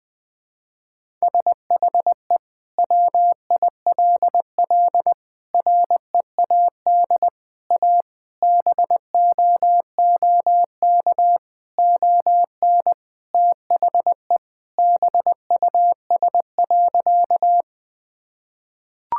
Morse code, heard as a keyed tone, about 20 wpm, 700 Hz. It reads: SHE WILL READ A BOOK ON THE BUS.
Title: Sentences from Top 1000 Words - Review Part 6 - 20wpm